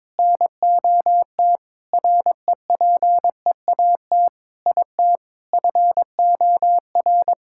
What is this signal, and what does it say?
Morse code, heard as a keyed tone, about 22 wpm, 700 Hz. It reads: NOT REPEAT IT FOR
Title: Sentences from Top 1600 Words 22wpm